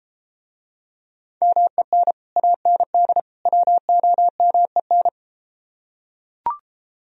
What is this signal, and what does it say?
Morse code, heard as a keyed tone, about 33 wpm, 700 Hz. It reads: MEN AND WOMEN